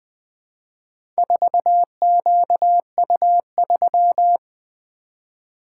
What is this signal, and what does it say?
Morse code, heard as a keyed tone, about 20 wpm, 700 Hz. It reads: 4QU3
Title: Four Letter-Number Combinations - 20wpm